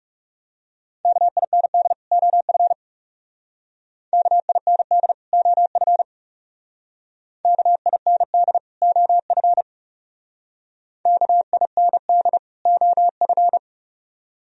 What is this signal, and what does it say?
Morse code, heard as a keyed tone, about 39 wpm, 700 Hz. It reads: KIND OF KIND OF KIND OF KIND OF